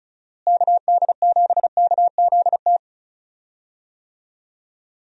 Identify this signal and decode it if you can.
Morse code, heard as a keyed tone, about 34 wpm, 700 Hz. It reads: KD7KZT
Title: USA Call Signs - 6 Characters - Repeated 2x - 35wpm